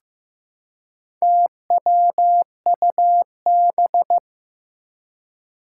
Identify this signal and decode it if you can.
Morse code, heard as a keyed tone, about 15 wpm, 700 Hz. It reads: TWUB
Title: Four Challenging Letter-Number Combinations 15wpm